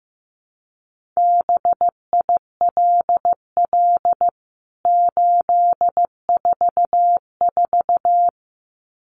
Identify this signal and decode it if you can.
Morse code, heard as a keyed tone, about 15 wpm, 700 Hz. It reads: BILL 844
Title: CWT Exchange 15wpm